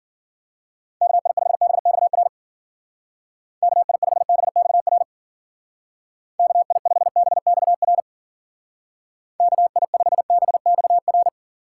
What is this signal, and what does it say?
Morse code, heard as a keyed tone, about 49 wpm, 700 Hz. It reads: KI5BXR KI5BXR KI5BXR KI5BXR
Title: Speed Racing - USA 2x3 Call Signs 40wpm